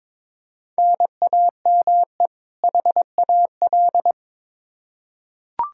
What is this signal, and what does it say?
Morse code, heard as a keyed tone, about 22 wpm, 700 Hz. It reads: NAME HAL
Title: Practice QSO - Exchange 1 - Beginning - 22wpm